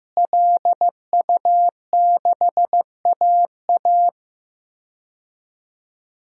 Morse code, heard as a keyed tone, about 15 wpm, 700 Hz. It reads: LU6AA
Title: DXCC-100 Call Signs - 5 Characters - Repeated 3x - 15wpm